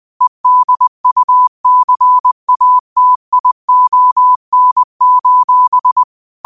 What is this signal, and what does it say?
Morse code, heard as a keyed tone, about 20 wpm, 1000 Hz. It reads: EDUCATION:
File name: qotdfiles / 20wpm